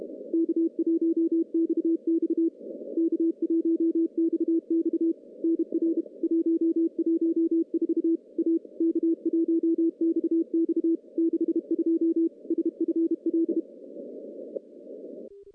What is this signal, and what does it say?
Morse code, heard as a keyed tone, about 32 wpm, 330 Hz. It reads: K1XX K1XX NR 114 A K1XX 62 SFL
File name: contests / CW-SS-k5tr- -0200